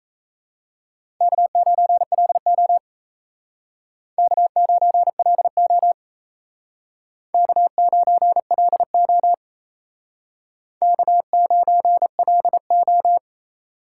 Morse code, heard as a keyed tone, about 35 wpm, 700 Hz. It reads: K9LO K9LO K9LO K9LO